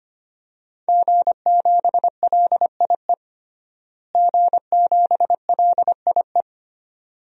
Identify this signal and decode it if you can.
Morse code, heard as a keyed tone, about 25 wpm, 700 Hz. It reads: G7LIE G7LIE
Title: DXCC-100 Call Signs - 5 Characters - Repeated 2x - 25wpm